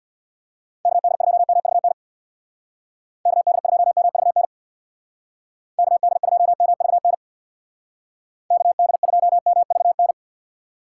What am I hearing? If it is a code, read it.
Morse code, heard as a keyed tone, about 65 wpm, 700 Hz. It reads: KD2MVN KD2MVN KD2MVN KD2MVN